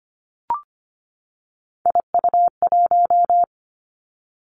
Morse code, heard as a keyed tone, about 25 wpm, 700 Hz. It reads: IU1
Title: Three Challenging Letter-Number Combinations 25wpm